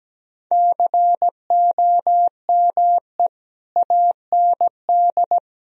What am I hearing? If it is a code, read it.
Morse code, heard as a keyed tone, about 17 wpm, 700 Hz. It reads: COME AND
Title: Sentences from Top 1000 Words - Review Part 1 - 17wpm